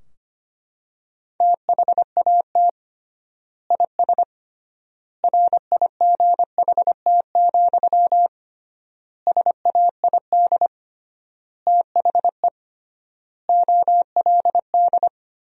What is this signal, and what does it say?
Morse code, heard as a keyed tone, about 25 wpm, 700 Hz. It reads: THAT IS RIGHT, SAID THE OLD
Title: 3x Word-Spacing - Sentences from Top 200 Words - 25wpm